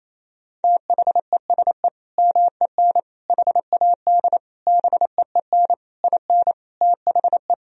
Morse code, heard as a keyed tone, about 28 wpm, 700 Hz. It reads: THESE MEN HAD BEEN IN THE
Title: Sentences from Top 1000 Words - Review Part 1 - 28wpm